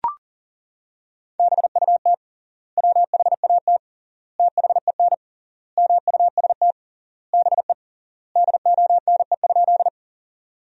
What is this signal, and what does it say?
Morse code, heard as a keyed tone, about 40 wpm, 700 Hz. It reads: BUT WHAT THEN MUST BE DONE?
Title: 3x Word-Spacing - Sentences from Top 500 Words - 40wpm